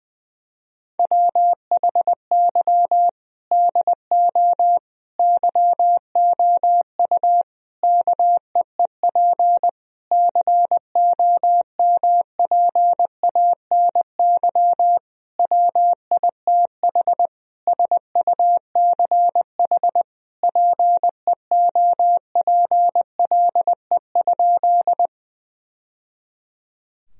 Morse code, heard as a keyed tone, about 21 wpm, 700 Hz. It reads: WHY DO YOU KEEP COMPANY WITH SUCH PEOPLE?
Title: Sentences from Top 1000 Words - Review Part 3 - 20wpm